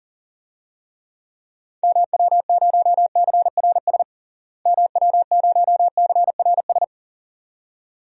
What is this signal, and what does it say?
Morse code, heard as a keyed tone, about 40 wpm, 700 Hz. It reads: MW0CRS MW0CRS
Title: DXCC-100 Call Signs - 6 Characters - Repeated 2x - 40wpm